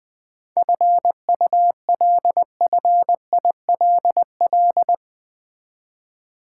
Morse code, heard as a keyed tone, about 20 wpm, 700 Hz. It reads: FULFILL
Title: 7 Letter Words - First 3 Letters Given - 20wpm